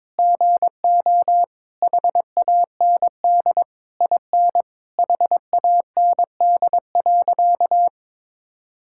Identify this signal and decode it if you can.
Morse code, heard as a keyed tone, about 22 wpm, 700 Hz. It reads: GO HAND IN HAND.
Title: Sentences from Top 1000 Words - Review Part 6 - 22wpm